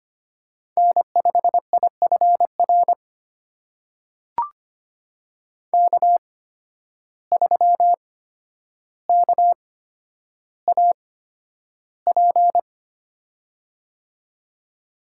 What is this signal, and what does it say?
Morse code, heard as a keyed tone, about 25 wpm, spaced as for 11 wpm, 700 Hz. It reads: N5IFR K3KAP
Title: USA Call Signs - 5 Characters - 8x Character Spacing - 25wpm